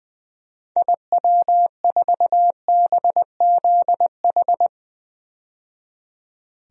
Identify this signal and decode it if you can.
Morse code, heard as a keyed tone, about 20 wpm, 700 Hz. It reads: IW4BZH
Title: DXCC-100 Call Signs - 6 Characters - 20wpm